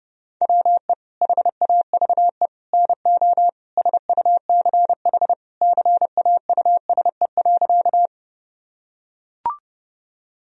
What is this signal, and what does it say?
Morse code, heard as a keyed tone, about 30 wpm, 700 Hz. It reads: WE HAVE NO SUCH CAUSE.
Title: Sentences from Top 1000 Words - Review Part 1 - 30wpm